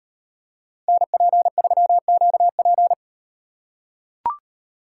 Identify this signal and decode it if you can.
Morse code, heard as a keyed tone, about 38 wpm, 700 Hz. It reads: NP3QP